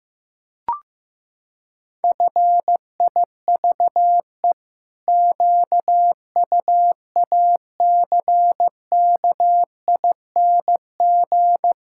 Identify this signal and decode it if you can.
Morse code, heard as a keyed tone, about 15 wpm, 700 Hz. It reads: FIVE QUACKING
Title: Pangram Challenge v2 - 15wpm